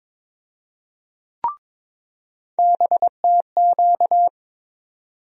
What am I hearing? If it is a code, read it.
Morse code, heard as a keyed tone, about 21 wpm, 700 Hz. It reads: BTQ
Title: USA Call Sign Postfixes - 3 Letters 22wpm